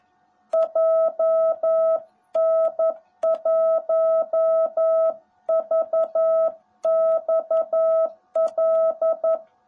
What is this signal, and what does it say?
Morse code, heard as a keyed tone, about 11 wpm, 660 Hz. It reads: JN1VXL
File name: jh8cbh / 2021cw / cw18b